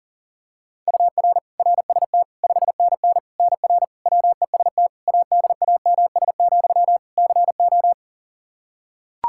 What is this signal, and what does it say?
Morse code, heard as a keyed tone, about 40 wpm, 700 Hz. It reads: UR RST 5NN NR WEST ADAMS, CO
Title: Practice QSO - Exchange 1 - Beginning - 40wpm